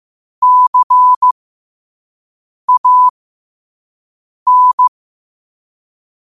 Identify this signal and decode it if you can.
Morse code, heard as a keyed tone, about 15 wpm, spaced as for 5 wpm, 1000 Hz. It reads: CAN